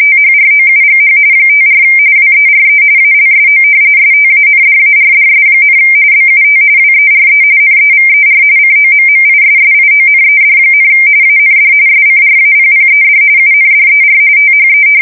RTTY, about 45.45 baud. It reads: THROUGH 54 WERE ALSO STANDARDS, BUT WERE NEVER COMMISSIONED, AND SCRAPPED UNDER THE
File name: fsk170 / battleships